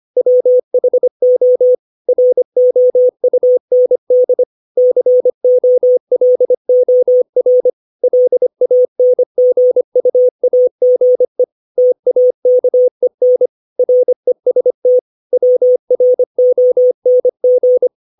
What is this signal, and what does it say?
Morse code, heard as a keyed tone, about 25 wpm, 500 Hz. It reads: WHO ROUND COLOR LANGUAGE TAKEN REST WRONG
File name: 632 words / 632 Words 25WPM